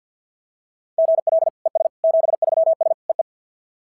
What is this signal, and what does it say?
Morse code, heard as a keyed tone, about 46 wpm, 660 Hz. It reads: GL ES 73S EE
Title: SOTA QSO Practice 50wpm